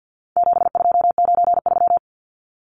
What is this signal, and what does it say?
Morse code, heard as a keyed tone, about 50 wpm, 700 Hz. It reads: 7293
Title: Numbers - 4-Digit 50wpm